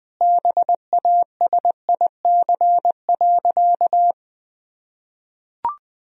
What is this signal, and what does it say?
Morse code, heard as a keyed tone, about 20 wpm, 700 Hz. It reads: BASIC.
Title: Sentences from Top 1000 Words - Review Part 3 - 20wpm